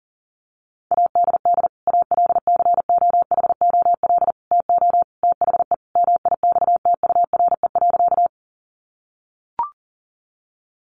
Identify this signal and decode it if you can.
Morse code, heard as a keyed tone, about 40 wpm, 700 Hz. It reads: ADD ALCOHOL TO THE MIXTURE.
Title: Sentences from Top 1500 Words 40wpm